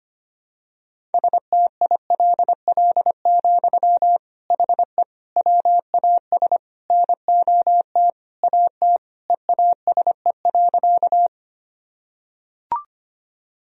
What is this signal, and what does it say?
Morse code, heard as a keyed tone, about 25 wpm, 700 Hz. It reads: STILL, HE WAS NOT AT EASE.
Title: Sentences from Top 1000 Words - Review Part 1 - 25wpm